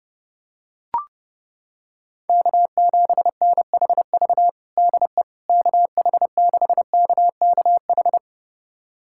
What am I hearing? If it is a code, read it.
Morse code, heard as a keyed tone, about 30 wpm, 700 Hz. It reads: K7NHV DE KH6KKH
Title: USA Callsign DE Callsign 30wpm